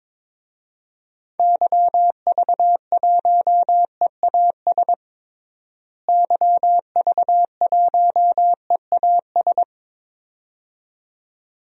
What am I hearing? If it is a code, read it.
Morse code, heard as a keyed tone, about 21 wpm, 700 Hz. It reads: YV1EAS YV1EAS
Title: DXCC-100 Call Signs - 6 Characters - Repeated 2x - 22wpm